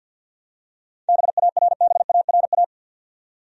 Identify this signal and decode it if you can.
Morse code, heard as a keyed tone, about 50 wpm, 700 Hz. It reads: BARBARA
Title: USA Common Names - 50wpm